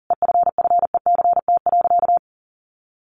Morse code, heard as a keyed tone, about 40 wpm, 700 Hz. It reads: EFFECT.
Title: Sentences from Top 1000 Words 40wpm